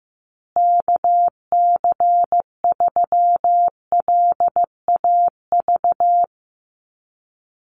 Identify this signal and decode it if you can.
Morse code, heard as a keyed tone, about 15 wpm, 700 Hz. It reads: KC3LAV
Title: USA Call Signs Repeated 3x 15wpm